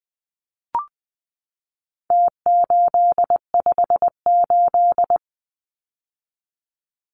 Morse code, heard as a keyed tone, about 20 wpm, 700 Hz. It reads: T858
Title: Numbers - Contest Style - 4-Digit 20wpm